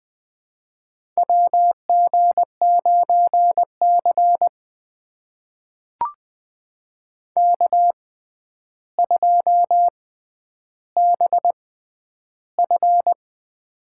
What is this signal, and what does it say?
Morse code, heard as a keyed tone, about 20 wpm, spaced as for 10 wpm, 700 Hz. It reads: WG9C K2BF
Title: USA Call Signs - 4 Characters - 6x Character Spacing - 20wpm